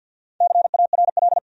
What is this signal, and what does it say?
Morse code, heard as a keyed tone, about 52 wpm, 700 Hz. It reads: CARL
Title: USA Common Names - Max 4 Letters - 50wpm